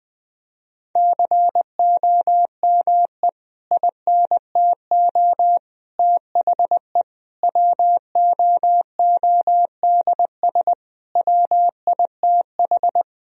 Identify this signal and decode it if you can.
Morse code, heard as a keyed tone, about 20 wpm, 700 Hz. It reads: COME INTO THE WOODS WITH
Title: Sentences from Top 1000 Words - Review Part 2 - 20wpm